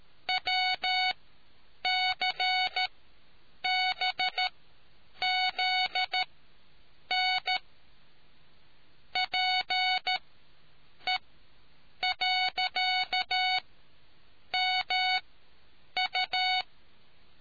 Morse code, heard as a keyed tone, about 13 wpm, spaced as for 7 wpm, 740 Hz. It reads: WCBZN PE.MU